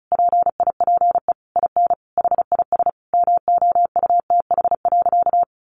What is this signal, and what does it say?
Morse code, heard as a keyed tone, about 34 wpm, 700 Hz. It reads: PIPE IN HIS MOUTH.
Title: Sentences from Top 1500 Words 35wpm